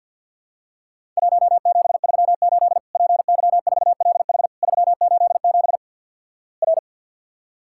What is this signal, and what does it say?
Morse code, heard as a keyed tone, about 50 wpm, 700 Hz. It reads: 1738 PY4LH 387 R
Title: Worked All Europe DX Contest - QTC Exchange - 50wpm